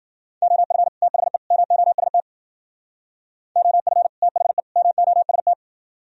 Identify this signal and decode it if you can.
Morse code, heard as a keyed tone, about 51 wpm, 700 Hz. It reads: OF THE MOST OF THE MOST